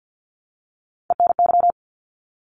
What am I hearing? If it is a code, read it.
Morse code, heard as a keyed tone, about 50 wpm, 700 Hz. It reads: ENY
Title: ARRL Sections 50wpm